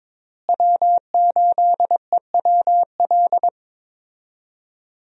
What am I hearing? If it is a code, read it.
Morse code, heard as a keyed tone, about 22 wpm, 700 Hz. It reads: W8EWL
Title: USA Call Signs - 5 Characters - Repeated 3x - 22wpm